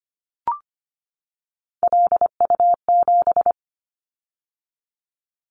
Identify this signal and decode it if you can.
Morse code, heard as a keyed tone, about 25 wpm, 700 Hz. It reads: LU7
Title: Callsign Prefixes for Top 100 DXCC - 25wpm